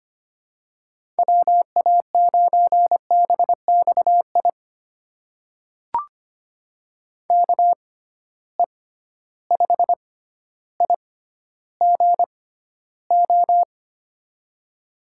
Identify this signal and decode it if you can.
Morse code, heard as a keyed tone, about 25 wpm, spaced as for 13 wpm, 700 Hz. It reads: WA9BXI KE5IGO
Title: USA Call Signs - 6 Characters - 6x Character Spacing - 25wpm